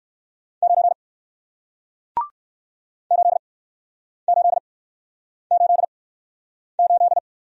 Morse code, heard as a keyed tone, about 57 wpm, spaced as for 9 wpm, 700 Hz. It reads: 9 8888